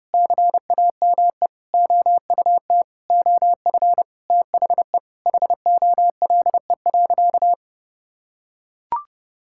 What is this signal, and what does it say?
Morse code, heard as a keyed tone, about 32 wpm, 700 Hz. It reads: CAME OUT OF THE HOLE.
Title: Sentences from Top 1600 Words 30wpm